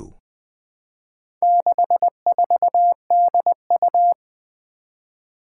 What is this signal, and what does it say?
Morse code, heard as a keyed tone, about 20 wpm, 700 Hz. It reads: 64DU
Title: Four Challenging Letter-Number Combinations 20wpm